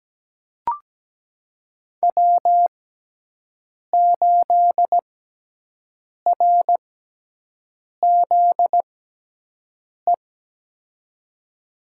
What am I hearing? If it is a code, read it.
Morse code, heard as a keyed tone, about 17 wpm, spaced as for 6 wpm, 700 Hz. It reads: W8RZE
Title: USA Call Signs - 5 Characters - 6x Character Spacing - 17wpm